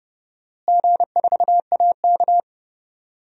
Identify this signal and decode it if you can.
Morse code, heard as a keyed tone, about 30 wpm, 700 Hz. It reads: G4AK
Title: DXCC-100 Call Signs - 4 Characters - Repeated 2x - 30wpm